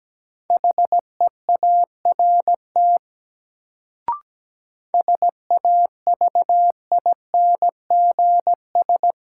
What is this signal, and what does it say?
Morse code, heard as a keyed tone, about 17 wpm, 700 Hz. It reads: HEART SAVINGS
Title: Sentences from Top 1200 Words - New Word Drill - 17wpm